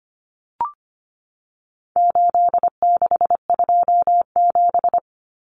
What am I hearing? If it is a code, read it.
Morse code, heard as a keyed tone, about 25 wpm, 700 Hz. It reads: 8627
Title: Numbers - 4-Digit 25wpm